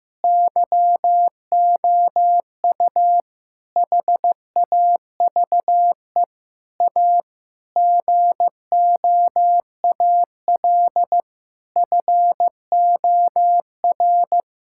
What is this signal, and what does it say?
Morse code, heard as a keyed tone, about 16 wpm, 700 Hz. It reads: YOU HAVE A GOAL FOR
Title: Sentences from Top 1000 Words - Review Part 5 - 15wpm